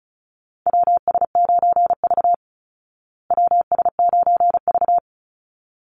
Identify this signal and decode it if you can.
Morse code, heard as a keyed tone, about 34 wpm, 700 Hz. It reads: WS9V WS9V
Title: Call Signs Repeated 3x - 35wpm